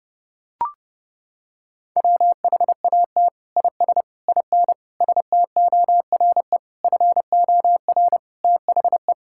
Morse code, heard as a keyed tone, about 30 wpm, 700 Hz. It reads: WHAT IS IN STORE FOR THE
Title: Sentences from Top 1000 Words - Review Part 5 - 30wpm